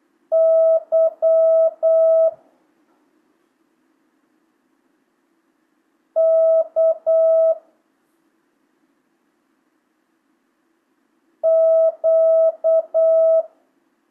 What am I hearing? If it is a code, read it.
Morse code, heard as a keyed tone, about 8 wpm, 650 Hz. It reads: Y K Q
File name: jh8cbh / 2025cw